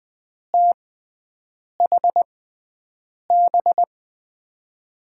Morse code, heard as a keyed tone, about 20 wpm, spaced as for 7 wpm, 700 Hz. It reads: THB
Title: USA Call Signs - 6x Character Spacing - 20wpm